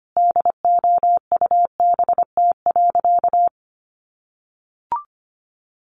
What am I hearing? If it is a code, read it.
Morse code, heard as a keyed tone, about 25 wpm, 700 Hz. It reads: DOUBT.
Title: Sentences from Top 1500 Words 25wpm